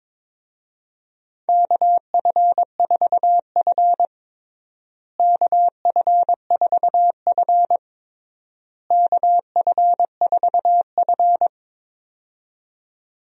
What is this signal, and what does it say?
Morse code, heard as a keyed tone, about 22 wpm, 700 Hz. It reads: KF4F KF4F KF4F
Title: USA Call Signs - 4 Characters - Repeated 3x - 22wpm